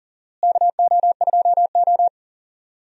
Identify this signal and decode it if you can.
Morse code, heard as a keyed tone, about 40 wpm, 700 Hz. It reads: KO2O
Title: USA Call Signs - 4 Characters - Repeated 3x - 40wpm